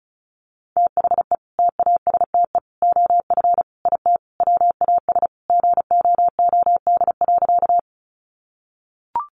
Morse code, heard as a keyed tone, about 34 wpm, 700 Hz. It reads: THE TASTE OF IT WAS GOOD.
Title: Sentences from Top 1100 Words 35wpm